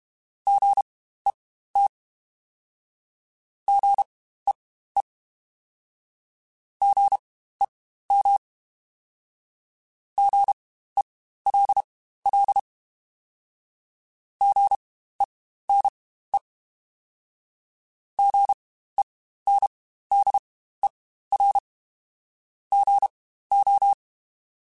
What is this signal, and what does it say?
Morse code, heard as a keyed tone, about 32 wpm, spaced as for 15 wpm, 800 Hz. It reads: GET GEE GEM GELL GENE GENDER GO
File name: Spaced Alphas / Spaced Alphas mp3s / Gs sp @30